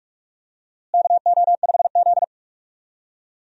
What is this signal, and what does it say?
Morse code, heard as a keyed tone, about 46 wpm, 700 Hz. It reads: KO5Z
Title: USA Call Signs - 4 Characters - Repeated 3x - 45wpm